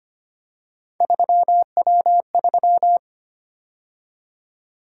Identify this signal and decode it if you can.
Morse code, heard as a keyed tone, about 25 wpm, 700 Hz. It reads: 3W3
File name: Three Letter-Number Combinations - 25wpm